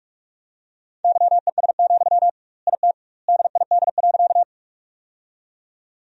Morse code, heard as a keyed tone, about 46 wpm, 700 Hz. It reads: YES, IT DID.
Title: 2x Word-Spacing - Sentences from Top 500 Words - 45wpm